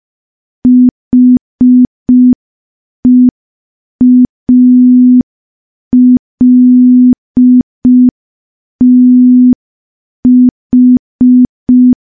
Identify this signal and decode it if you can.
Morse code, heard as a keyed tone, about 5 wpm, 250 Hz. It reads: HEALTH
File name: morse / morse-training-05WPM